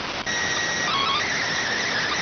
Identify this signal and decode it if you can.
SSTV (slow-scan television), PD 290.